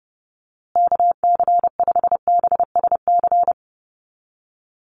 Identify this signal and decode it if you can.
Morse code, heard as a keyed tone, about 30 wpm, 700 Hz. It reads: KC5BSC
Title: USA Call Signs Repeated 3x 30wpm